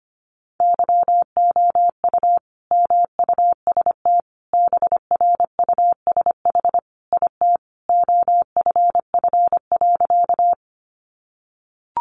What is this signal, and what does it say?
Morse code, heard as a keyed tone, about 25 wpm, 700 Hz. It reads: YOU MUST BRUSH IT OFF.
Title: Sentences from Top 1100 Words 25wpm